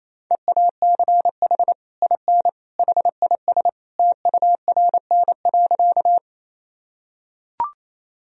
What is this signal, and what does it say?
Morse code, heard as a keyed tone, about 27 wpm, 700 Hz. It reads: EACH IN HIS TURN.
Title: Sentences from Top 1000 Words - Review Part 1 - 28wpm